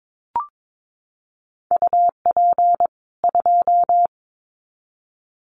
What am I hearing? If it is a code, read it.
Morse code, heard as a keyed tone, about 22 wpm, 700 Hz. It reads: UP 2
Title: Common QSO Phrases 22wpm